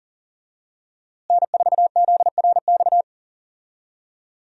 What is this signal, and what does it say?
Morse code, heard as a keyed tone, about 40 wpm, 700 Hz. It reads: N4ZRX
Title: USA Call Signs - 5 Characters - Repeated 3x - 40wpm